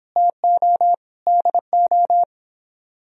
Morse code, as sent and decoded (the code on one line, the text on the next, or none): - --- / -.. ---
TO DO